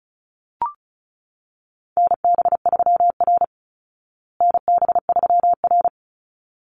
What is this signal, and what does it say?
Morse code, heard as a keyed tone, about 34 wpm, 700 Hz. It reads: NB3R NB3R
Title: Call Signs Repeated 3x - 35wpm